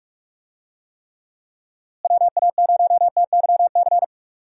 Morse code, heard as a keyed tone, about 43 wpm, 700 Hz. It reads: WA0TYC
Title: USA Call Signs - 6 Characters - Repeated 3x - 45wpm